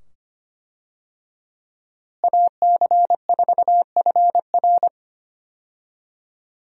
Morse code, heard as a keyed tone, about 25 wpm, 700 Hz. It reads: AC4FR
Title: USA Call Signs - 5 Characters - Repeated 2x - 25wpm